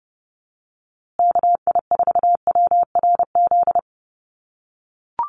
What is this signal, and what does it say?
Morse code, heard as a keyed tone, about 30 wpm, 700 Hz. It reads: KI4WRZ
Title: USA Call Signs Repeated 3x 30wpm